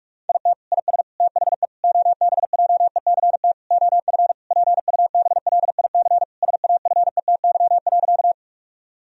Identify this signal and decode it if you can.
Morse code, heard as a keyed tone, about 41 wpm, 700 Hz. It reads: IT IS THE OBJECT OF PUBLIC SAFETY.
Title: Sentences from Top 1000 Words - Review Part 4 - 45wpm